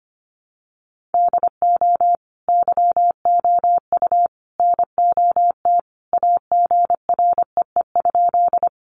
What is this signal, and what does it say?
Morse code, heard as a keyed tone, about 25 wpm, 700 Hz. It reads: DO YOU NOT AGREE?
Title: Sentences from Top 1100 Words 25wpm